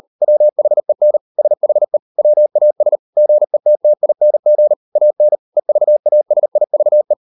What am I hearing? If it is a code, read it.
Morse code, heard as a keyed tone, about 40 wpm, 600 Hz. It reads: WHEN SHE WAS GETTING AN EVASIVE